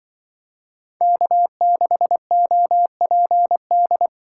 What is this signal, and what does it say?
Morse code, heard as a keyed tone, about 25 wpm, 700 Hz. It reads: K6OPD